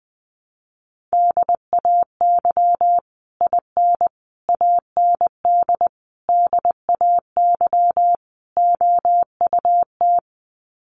Morse code, heard as a keyed tone, about 20 wpm, 700 Hz. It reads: DAY IN AND DAY OUT
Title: Sets of 5 Words Encore2 20wpm